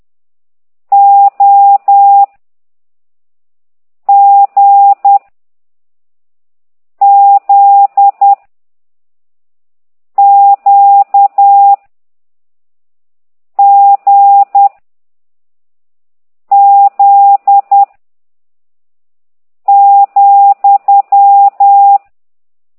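Morse code, heard as a keyed tone, about 10 wpm, 800 Hz. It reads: O G Z Q G Z ,
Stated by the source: Morse code lesson 5 exercise 10 by CT2IDW, containing only characters M, O, G, Z, Q and comma.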